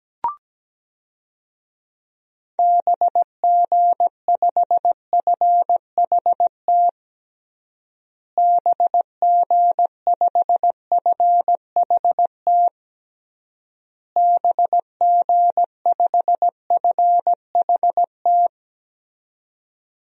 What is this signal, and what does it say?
Morse code, heard as a keyed tone, about 17 wpm, 700 Hz. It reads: BG5FHT BG5FHT BG5FHT
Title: DXCC-100 Call Signs - 6 Characters - Repeated 3x - 17wpm